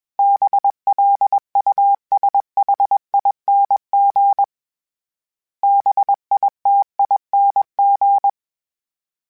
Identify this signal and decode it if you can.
Morse code, heard as a keyed tone, about 21 wpm, 800 Hz. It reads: BLUSHING BITING